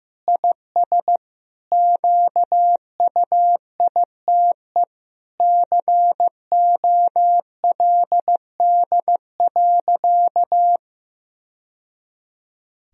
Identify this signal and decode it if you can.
Morse code, heard as a keyed tone, about 15 wpm, 700 Hz. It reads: IS QUITE COLD.
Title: Sentences from Top 1000 Words - Review Part 4 - 15wpm